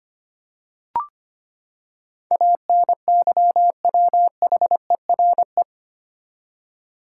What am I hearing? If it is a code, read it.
Morse code, heard as a keyed tone, about 25 wpm, 700 Hz. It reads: ANYWHERE
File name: Eight Letter Words - 25wpm